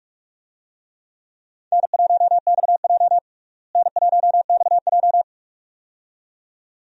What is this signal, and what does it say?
Morse code, heard as a keyed tone, about 45 wpm, 700 Hz. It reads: N1XJ N1XJ
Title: USA Call Signs - 4 Characters - Repeated 2x - 45wpm